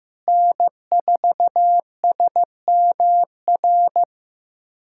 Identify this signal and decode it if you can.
Morse code, heard as a keyed tone, about 16 wpm, 700 Hz. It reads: N4SMR
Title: USA Callsign DE Callsign - 5 Characters - 15wpm